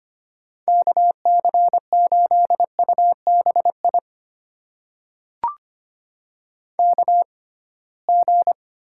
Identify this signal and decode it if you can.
Morse code, heard as a keyed tone, about 25 wpm, spaced as for 19 wpm, 700 Hz. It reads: KC8UBI KG
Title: USA Call Signs - 6 Characters - 6x Character Spacing - 25wpm